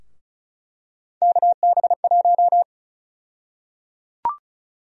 Morse code, heard as a keyed tone, about 34 wpm, 700 Hz. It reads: KB1
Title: USA Call Sign Prefixes - 35wpm